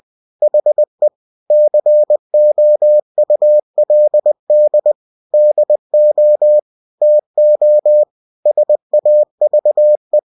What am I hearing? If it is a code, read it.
Morse code, heard as a keyed tone, about 20 wpm, 600 Hz. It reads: HE COULD DO TO SAVE
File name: cw-books / winnie-the-pooh / winnie-20wpm-page-0093